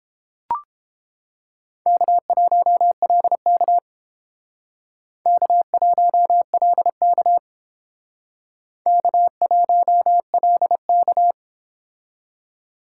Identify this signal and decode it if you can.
Morse code, heard as a keyed tone, about 30 wpm, 700 Hz. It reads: K1LK K1LK K1LK